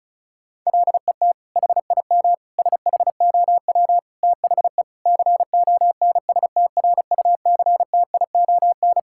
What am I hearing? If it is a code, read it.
Morse code, heard as a keyed tone, about 36 wpm, 700 Hz. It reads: LET HIM SHOW THE CONSTRUCTION
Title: Sentences from Top 1600 Words 35wpm